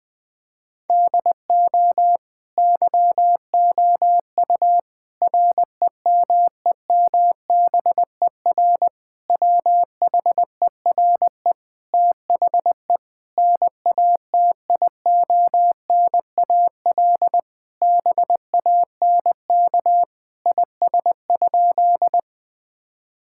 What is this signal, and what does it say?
Morse code, heard as a keyed tone, about 20 wpm, 700 Hz. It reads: DO YOU REMEMBER WHERE THE NATIONAL BANK IS?
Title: Sentences from Top 1000 Words - Review Part 6 - 20wpm